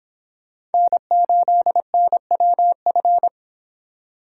Morse code, heard as a keyed tone, about 28 wpm, 700 Hz. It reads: N8NWF